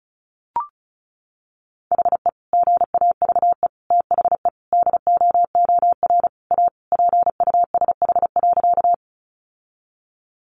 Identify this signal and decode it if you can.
Morse code, heard as a keyed tone, about 34 wpm, 700 Hz. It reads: HE GAVE THE DOOR A PUSH.
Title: Sentences from Top 1300 Words 35wpm